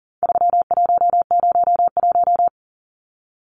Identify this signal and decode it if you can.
Morse code, heard as a keyed tone, about 40 wpm, 700 Hz. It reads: 3101
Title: Numbers - 4-Digit 40wpm